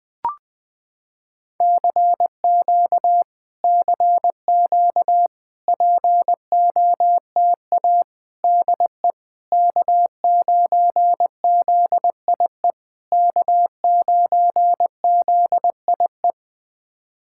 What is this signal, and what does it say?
Morse code, heard as a keyed tone, about 20 wpm, 700 Hz. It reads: CQ CQ POTA DE K9ZIE K9ZIE
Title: POTA QSO Practice - 20wpm